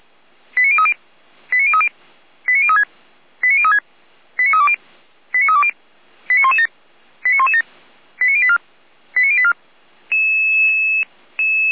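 Five-tone selective call (ZVEI), heard as quick runs of five tones, tones of about 70 ms. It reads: 89030 89030 89036 89036 89320 89320 89117 89117 89084 89084
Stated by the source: ZVEI five-tone selective call 89050, received by SDR.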